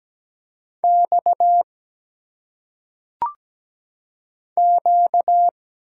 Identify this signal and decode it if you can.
Morse code, heard as a keyed tone, about 17 wpm, 700 Hz. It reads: X Q
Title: Single Letters - Phonetic Pronunciation 17wpm